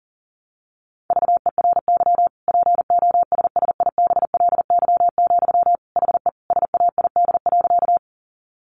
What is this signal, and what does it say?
Morse code, heard as a keyed tone, about 40 wpm, 700 Hz. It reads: VERY POSSIBLY, HE SAID.
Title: Sentences from Top 1300 Words 40wpm